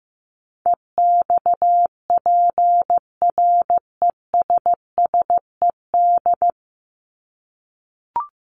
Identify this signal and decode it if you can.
Morse code, heard as a keyed tone, about 14 wpm, 700 Hz. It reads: EXPRESSED
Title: ED Suffix - Top 100 Words with Base Words 15wpm